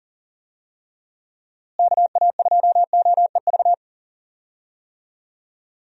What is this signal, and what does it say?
Morse code, heard as a keyed tone, about 40 wpm, 700 Hz. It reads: KA2OEV
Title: USA Call Signs - 6 Characters - Repeated 2x - 40wpm